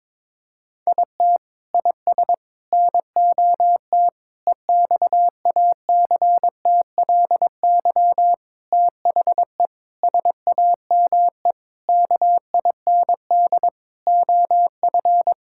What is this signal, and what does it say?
Morse code, heard as a keyed tone, about 22 wpm, 700 Hz. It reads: IT IS NOT EXACTLY THE SAME KIND OF
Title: Sentences from Top 1000 Words - Review Part 6 - 22wpm